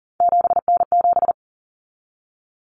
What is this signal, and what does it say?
Morse code, heard as a keyed tone, about 40 wpm, 700 Hz. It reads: 7N7
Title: Common Cut Numbers - 3-Digit 40wpm